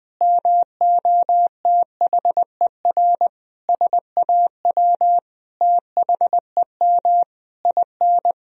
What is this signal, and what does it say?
Morse code, heard as a keyed tone, about 20 wpm, 700 Hz. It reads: MOTHER SAW THEM IN
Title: Sentences from Top 1000 Words - Review Part 3 - 20wpm